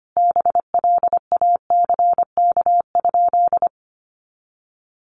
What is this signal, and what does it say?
Morse code, heard as a keyed tone, about 25 wpm, 700 Hz. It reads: BLACK?
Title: Sentences from Top 400 Words 25wpm